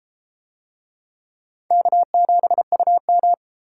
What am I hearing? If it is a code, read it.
Morse code, heard as a keyed tone, about 31 wpm, 700 Hz. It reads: K7UM